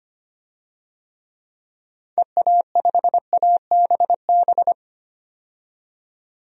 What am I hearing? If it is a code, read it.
Morse code, heard as a keyed tone, about 25 wpm, 700 Hz. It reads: EA5ABB
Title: DXCC-100 Call Signs - 6 Characters - 25wpm